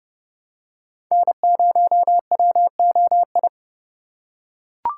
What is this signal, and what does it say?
Morse code, heard as a keyed tone, about 30 wpm, 700 Hz. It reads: N0WOI
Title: K1USN SST QSO Practice - 30wpm